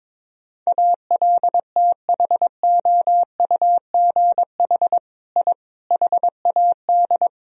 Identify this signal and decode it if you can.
Morse code, heard as a keyed tone, about 22 wpm, 700 Hz. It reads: ALTHOUGH I HAD
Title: Sentences from Top 1000 Words - Review Part 3 - 22wpm